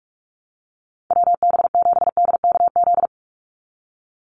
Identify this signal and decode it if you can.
Morse code, heard as a keyed tone, about 44 wpm, 700 Hz. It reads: WB7DKZ
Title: Call Signs - 45wpm